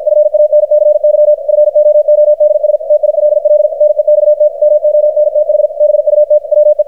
Morse code, heard as a keyed tone, about 53 wpm, 600 Hz, with radio noise. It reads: FANCY FOODS TILL NEXT MONTH BUT WE
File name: archived / CRSnet